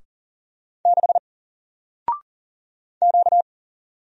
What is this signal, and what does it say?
Morse code, heard as a keyed tone, about 40 wpm, 700 Hz. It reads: 6 Q
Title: Single Letter-Number 40wpm